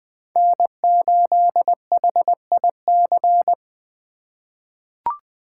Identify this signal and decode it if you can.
Morse code, heard as a keyed tone, about 21 wpm, 700 Hz. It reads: N8HIC
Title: USA Call Signs - 8x Character Spacing - 20wpm